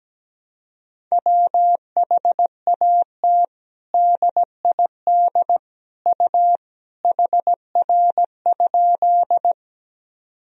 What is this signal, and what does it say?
Morse code, heard as a keyed tone, about 17 wpm, 700 Hz. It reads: WHAT DID U HR?
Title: Sentences from Top 300 Words and Abbreviations 17wpm